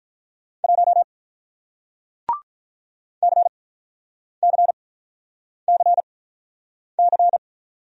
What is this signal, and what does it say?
Morse code, heard as a keyed tone, about 46 wpm, spaced as for 9 wpm, 700 Hz. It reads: 1 CCCC